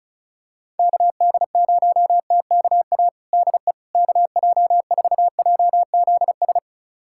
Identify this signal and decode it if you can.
Morse code, heard as a keyed tone, about 36 wpm, 700 Hz. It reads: KD0TKA DE KJ4JZS
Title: USA Callsign DE Callsign - 6 Characters - 35wpm